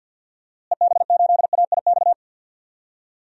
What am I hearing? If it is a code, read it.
Morse code, heard as a keyed tone, about 53 wpm, 700 Hz. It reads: EB8AIX
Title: DXCC-100 Call Signs - 6 Characters - Repeated 3x - 50wpm